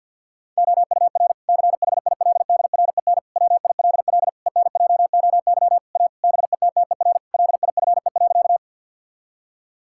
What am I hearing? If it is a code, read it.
Morse code, heard as a keyed tone, about 50 wpm, 700 Hz. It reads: OUR CHILDREN WILL ENJOY A BETTER LIFE.
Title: Sentences from Top 1000 Words - Review Part 5 - 50wpm